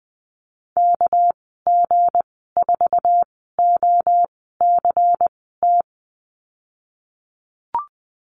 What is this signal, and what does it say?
Morse code, heard as a keyed tone, about 20 wpm, spaced as for 15 wpm, 700 Hz. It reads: KG4OCT
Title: 219 Lesson 35g - Number 0 - Call Signs 20wpm